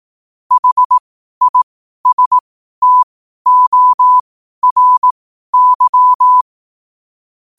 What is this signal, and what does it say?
Morse code, heard as a keyed tone, about 18 wpm, spaced as for 12 wpm, 1000 Hz. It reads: HISTORY
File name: qotdfiles / 12wpm